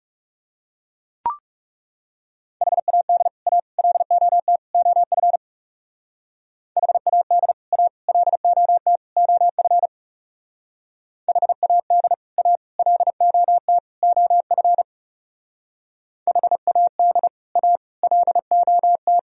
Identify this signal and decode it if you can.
Morse code, heard as a keyed tone, about 37 wpm, 690 Hz. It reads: HAD A LOT OF HAD A LOT OF HAD A LOT OF HAD A LOT
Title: Speed-Racing - Sets of 4 Words 30wpm